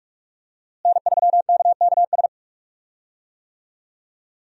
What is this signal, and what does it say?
Morse code, heard as a keyed tone, about 41 wpm, 700 Hz. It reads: N3KKS
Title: USA Call Signs - 5 Characters - Repeated 3x - 45wpm